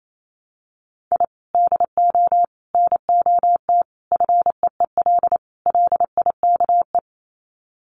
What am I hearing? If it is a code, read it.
Morse code, heard as a keyed tone, about 28 wpm, 700 Hz. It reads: I DO NOT FEEL LIKE
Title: Sets of 5 Words Encore 28wpm